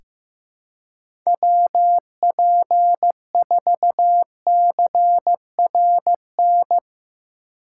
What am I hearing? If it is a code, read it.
Morse code, heard as a keyed tone, about 16 wpm, 700 Hz. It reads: WP4CRN
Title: USA Call Signs - 6 Characters - 4x Character Spacing - 15wpm